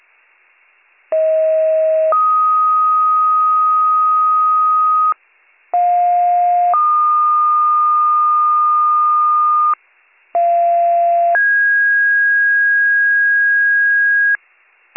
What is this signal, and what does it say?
Two-tone page, heard as a one-second tone, then a three-second tone, about 640 then 1250 Hz, about 710 then 1220 Hz, about 690 then 1650 Hz, with noise. It is heard from 1.1 to 14.4 s.